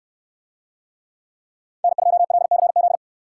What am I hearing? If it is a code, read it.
Morse code, heard as a keyed tone, about 69 wpm, 700 Hz. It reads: N2LCZ